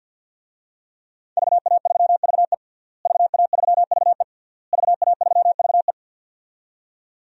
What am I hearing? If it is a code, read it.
Morse code, heard as a keyed tone, about 51 wpm, 700 Hz. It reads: VA3VE VA3VE VA3VE
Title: DXCC-100 Call Signs - 5 Characters - Repeated 3x - 50wpm